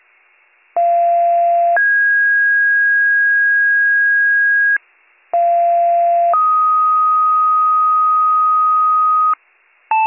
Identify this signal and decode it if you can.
Two-tone page, heard as a one-second tone, then a three-second tone, about 690 then 1650 Hz, about 690 then 1250 Hz, with noise. It is heard from 0.8 to 9.3 s.